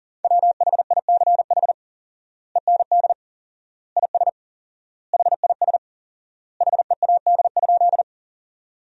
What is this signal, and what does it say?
Morse code, heard as a keyed tone, about 40 wpm, 700 Hz. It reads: WHICH END IS HIS HEAD?
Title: 4x Word-Spacing - Sentences from Top 300 Words - 40wpm